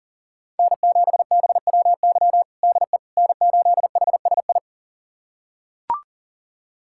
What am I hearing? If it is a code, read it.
Morse code, heard as a keyed tone, about 40 wpm, 690 Hz. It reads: N7BWY DE N8HSI
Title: USA Callsign DE Callsign - 5 Characters - 40wpm